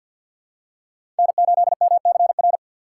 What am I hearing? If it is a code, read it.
Morse code, heard as a keyed tone, about 50 wpm, 700 Hz. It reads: N8MCR
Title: USA Call Signs - 5 Characters - 4x Character Spacing - 50wpm